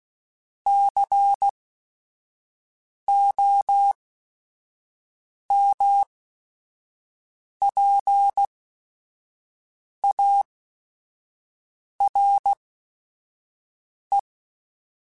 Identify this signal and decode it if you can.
Morse code, heard as a keyed tone, about 16 wpm, spaced as for 5 wpm, 800 Hz. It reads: COMPARE